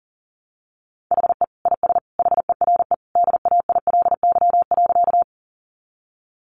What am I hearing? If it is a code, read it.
Morse code, heard as a keyed tone, about 40 wpm, 700 Hz. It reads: HE IS HERE DAILY.
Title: Sentences from Top 1100 Words 40wpm